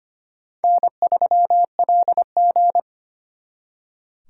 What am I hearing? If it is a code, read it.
Morse code, heard as a keyed tone, about 33 wpm, 700 Hz. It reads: N3LG